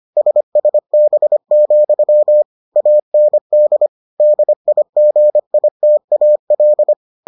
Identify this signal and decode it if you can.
Morse code, heard as a keyed tone, about 25 wpm, 600 Hz. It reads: SSB, AND DIGITAL